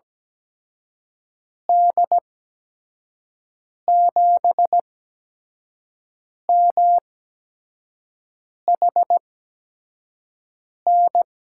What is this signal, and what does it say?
Morse code, heard as a keyed tone, about 17 wpm, spaced as for 5 wpm, 700 Hz. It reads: D7MHN
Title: USA Call Signs - 6 Characters - 8x Character Spacing - 17wpm